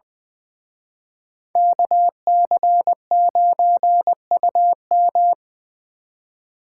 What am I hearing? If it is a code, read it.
Morse code, heard as a keyed tone, about 24 wpm, 700 Hz. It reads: KC9UM
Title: Speed Racing - USA 2x2 Call Signs 20wpm